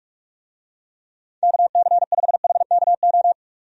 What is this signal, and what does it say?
Morse code, heard as a keyed tone, about 42 wpm, 700 Hz. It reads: KC5HKO
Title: USA Call Signs - 6 Characters - 45wpm